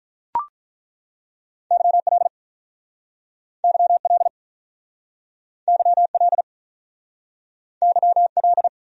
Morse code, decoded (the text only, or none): YL YL YL YL